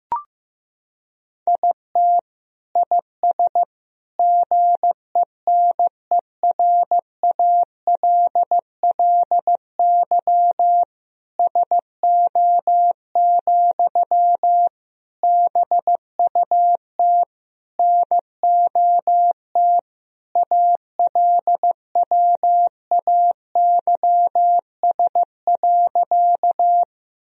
Morse code, heard as a keyed tone, about 15 wpm, 700 Hz. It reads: IT IS GENERALLY SO, BUT NOT ALWAYS.
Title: Sentences from Top 1000 Words - Review Part 4 - 15wpm